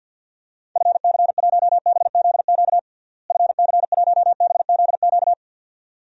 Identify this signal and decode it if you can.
Morse code, heard as a keyed tone, about 50 wpm, 700 Hz. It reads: FC1BZQ FC1BZQ
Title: DXCC-100 Call Signs - 6 Characters - Repeated 2x - 50wpm